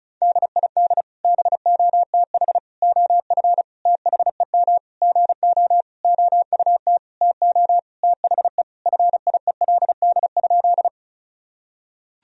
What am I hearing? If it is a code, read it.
Morse code, heard as a keyed tone, about 34 wpm, 700 Hz. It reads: DID BOTH OF THEM GO OUT TO THE FIELD?
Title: Sentences from Top 1000 Words - Review Part 2 - 35wpm